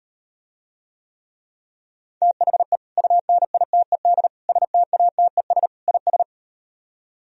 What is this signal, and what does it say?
Morse code, heard as a keyed tone, about 38 wpm, 700 Hz. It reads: THE UNITED STATES IS